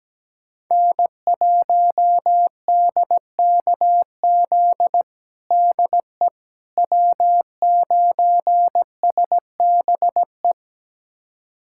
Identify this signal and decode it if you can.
Morse code, heard as a keyed tone, about 17 wpm, 700 Hz. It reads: N1DKZ DE W9SBE
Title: USA Callsign DE Callsign - 5 Characters - 17wpm